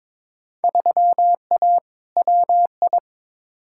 Morse code, heard as a keyed tone, about 22 wpm, 700 Hz. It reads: 3A WI
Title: Field Day Exchanges - 22wpm